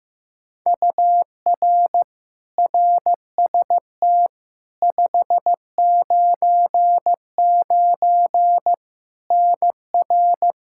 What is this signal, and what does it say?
Morse code, heard as a keyed tone, about 15 wpm, 700 Hz. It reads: UR RST 599 NR
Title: Practice QSO - Exchange 1 - Beginning - 15wpm